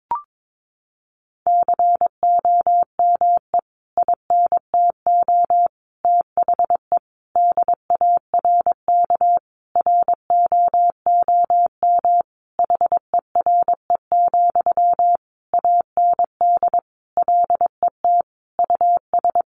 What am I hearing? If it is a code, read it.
Morse code, heard as a keyed tone, about 22 wpm, 700 Hz. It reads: COME INTO THE DARK ROOM HERE, AND LET US
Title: Sentences from Top 1000 Words 22wpm